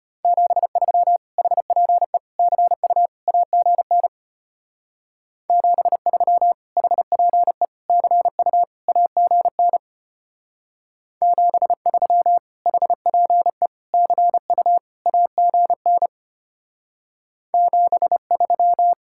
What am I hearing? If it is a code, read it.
Morse code, heard as a keyed tone, about 32 wpm, 700 Hz. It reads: 73 HPE CU AGN 73 HPE CU AGN 73 HPE CU AGN 73